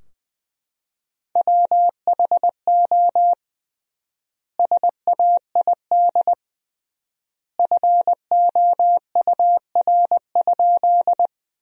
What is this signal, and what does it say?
Morse code, heard as a keyed tone, about 20 wpm, 700 Hz. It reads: WHO SAID FOUR?
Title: 3x Word-Spacing - Sentences from Top 300 Words - 20wpm